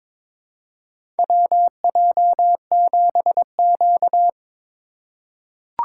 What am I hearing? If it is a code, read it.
Morse code, heard as a keyed tone, about 22 wpm, 700 Hz. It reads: WJ7Q
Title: USA Call Signs - 4 Characters - 3x Character Spacing - 22wpm